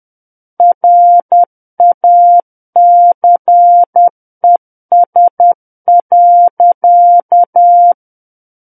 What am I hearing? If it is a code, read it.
Morse code, heard as a keyed tone, about 10 wpm, 700 Hz. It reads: RACES.
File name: Book One / 01-Chapter 1 - 10 WPM